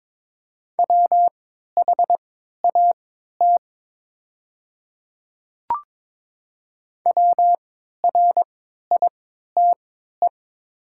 Morse code, heard as a keyed tone, about 22 wpm, spaced as for 13 wpm, 690 Hz. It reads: WHAT WRITE
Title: 3x Character Spacing - Top 100 Words - 22wpm